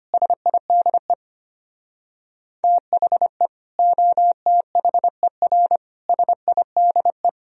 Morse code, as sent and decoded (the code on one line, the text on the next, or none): ... .. -.. / - .... / --- - .... .-. / ... .. -.. .
SIDE THE OTHER SIDE